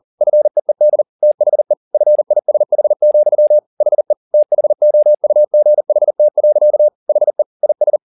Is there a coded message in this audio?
Yes: Morse code from 0.2 to 8.0 s, heard as a keyed tone, about 40 wpm, 600 Hz. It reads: FEED THE FISH, HE THOUGHT. HE IS